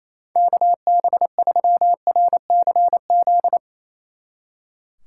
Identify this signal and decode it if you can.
Morse code, heard as a keyed tone, about 34 wpm, 700 Hz. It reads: KB3RCZ
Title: Speed Racing - USA 2x3 Call Signs 28wpm